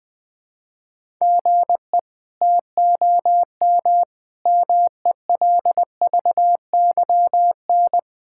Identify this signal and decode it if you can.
Morse code, heard as a keyed tone, about 20 wpm, 700 Hz. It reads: GE TOM MELVYN
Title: K1USN SST QSO Practice - 20wpm